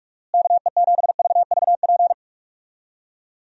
Morse code, heard as a keyed tone, about 46 wpm, 700 Hz. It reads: KE7VVP
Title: USA Call Signs - 6 Characters - Repeated 3x - 45wpm